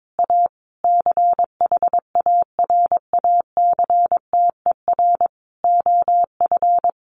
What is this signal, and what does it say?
Morse code, heard as a keyed tone, about 22 wpm, 700 Hz. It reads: A CHARACTER OF
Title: Sentences from Top 1200 Words 22wpm